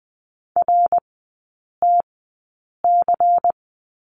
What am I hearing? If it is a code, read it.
Morse code, heard as a keyed tone, about 20 wpm, spaced as for 8 wpm, 690 Hz. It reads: RTC
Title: 145 Lesson 24e - Letter y - Three Characters 20wpm